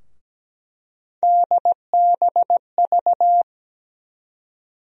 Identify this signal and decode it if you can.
Morse code, heard as a keyed tone, about 17 wpm, 700 Hz. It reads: DBV
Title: USA Call Sign Postfixes - 3 Letters 17wpm